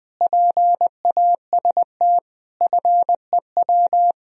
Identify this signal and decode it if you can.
Morse code, heard as a keyed tone, about 20 wpm, 700 Hz. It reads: PAST FEW
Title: Speed-Racing - Sets of 5 Words 20wpm